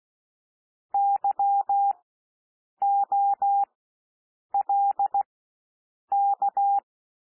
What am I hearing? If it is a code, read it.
Morse code, heard as a keyed tone, about 16 wpm, spaced as for 7 wpm, 800 Hz. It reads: YOLK
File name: cwmp3 / W1 SOUND WORDS 16 wpm